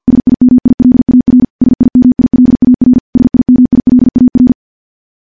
Morse code, heard as a keyed tone, about 51 wpm, 250 Hz. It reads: SIMILAR SIMILAR SIMILAR
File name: morse / morse-training-50WPM